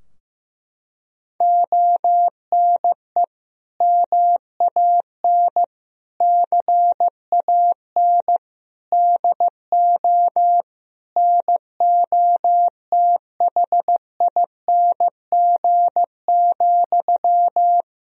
Morse code, heard as a keyed tone, about 15 wpm, 700 Hz. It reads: ONE MAN CAN DO NOTHING,
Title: Sentences from Top 1000 Words - Review Part 4 - 15wpm